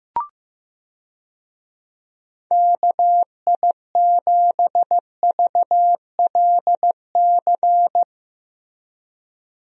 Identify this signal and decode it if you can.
Morse code, heard as a keyed tone, about 15 wpm, 700 Hz. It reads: KI7VLC
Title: USA Call Signs - 6 Characters - Repeated 3x - 15wpm